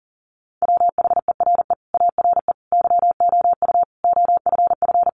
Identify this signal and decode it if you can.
Morse code, heard as a keyed tone, about 37 wpm, 700 Hz. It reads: WHERE ARE YOU OFF
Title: Sentences from Top 400 Words 40wpm